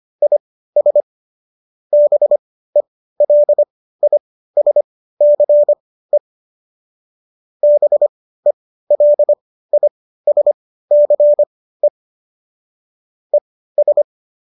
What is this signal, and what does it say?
Morse code, heard as a keyed tone, about 25 wpm, spaced as for 15 wpm, 590 Hz. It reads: IS BELISCE BELISCE ES
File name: qso219 15